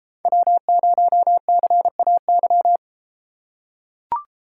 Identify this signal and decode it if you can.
Morse code, heard as a keyed tone, about 32 wpm, 700 Hz. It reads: W0CAY